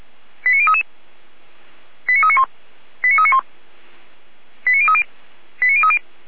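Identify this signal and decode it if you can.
Five-tone selective call (ZVEI), heard as quick runs of five tones, tones of about 80 ms. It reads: 89033 89391 89391 89030 89030
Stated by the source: ZVEI five-tone selective call received by SDR, logged on the Feuerwehr Orlamünde server.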